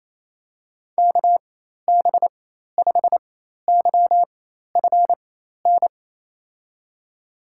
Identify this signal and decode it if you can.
Morse code, heard as a keyed tone, about 28 wpm, spaced as for 13 wpm, 700 Hz. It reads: KB5YFN
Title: USA Call Signs - 6 Characters - 4x Character Spacing - 28wpm